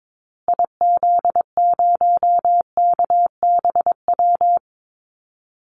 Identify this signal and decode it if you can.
Morse code, heard as a keyed tone, about 22 wpm, 700 Hz. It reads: IZ0KBW
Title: Call Signs Repeated 2x - 22wpm